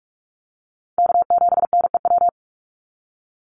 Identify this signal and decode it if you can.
Morse code, heard as a keyed tone, about 46 wpm, 700 Hz. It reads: K7NEW
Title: NA SOTA Chaser Callsigns 45wpm